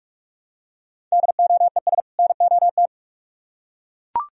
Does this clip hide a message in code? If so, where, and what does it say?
Yes: Morse code from 1.1 to 2.9 s, heard as a keyed tone, about 42 wpm, 700 Hz. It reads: DOES NOT